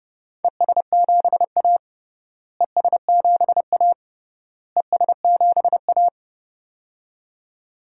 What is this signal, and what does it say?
Morse code, heard as a keyed tone, about 29 wpm, 700 Hz. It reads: ES7A ES7A ES7A
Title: DXCC-100 Call Signs - 4 Characters - Repeated 3x - 30wpm